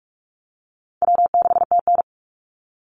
Morse code, heard as a keyed tone, about 44 wpm, 700 Hz. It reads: W6TN
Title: Easy Call Signs - 45wpm